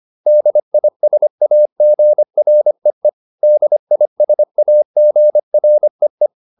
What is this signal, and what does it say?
Morse code, heard as a keyed tone, about 25 wpm, 600 Hz. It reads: DISAGREE DISAGREE
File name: DIS8 25